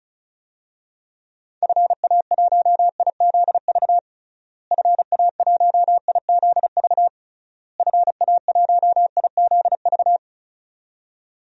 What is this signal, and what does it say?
Morse code, heard as a keyed tone, about 34 wpm, 700 Hz. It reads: FA1IZV FA1IZV FA1IZV
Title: DXCC-100 Call Signs - 6 Characters - Repeated 3x - 35wpm